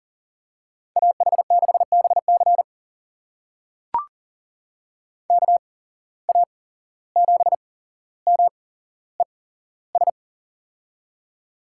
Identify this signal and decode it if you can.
Morse code, heard as a keyed tone, about 40 wpm, spaced as for 16 wpm, 700 Hz. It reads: AH6BC KA7MES
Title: USA Call Signs - 8x Character Spacing - 40wpm